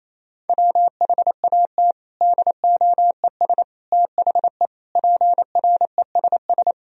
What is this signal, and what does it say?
Morse code, heard as a keyed tone, about 28 wpm, 700 Hz. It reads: WHAT DOES THE PRESS
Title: Sentences from Top 1000 Words - Review Part 3 - 28wpm